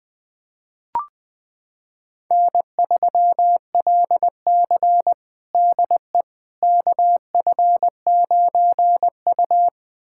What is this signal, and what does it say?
Morse code, heard as a keyed tone, about 20 wpm, 700 Hz. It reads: N3LC DE KF9U
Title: USA Callsign DE Callsign - 4 Characters - 20wpm